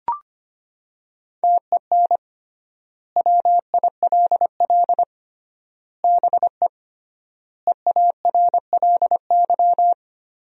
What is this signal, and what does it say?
Morse code, heard as a keyed tone, about 25 wpm, 700 Hz. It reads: TEN WILL BE EARLY
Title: 3x Word-Spacing - Sentences from Top 400 Words - 25wpm